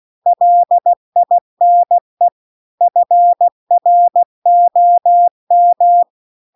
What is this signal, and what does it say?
Morse code, heard as a keyed tone, about 16 wpm, 700 Hz. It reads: LINE FROM
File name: CW-The-Big-Twoheaded-River-2-16WPM